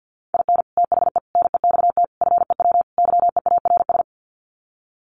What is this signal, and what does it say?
Morse code, heard as a keyed tone, about 50 wpm, 700 Hz. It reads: IN THE NEXT FEW YEARS
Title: Sets of 5 Words Encore 50wpm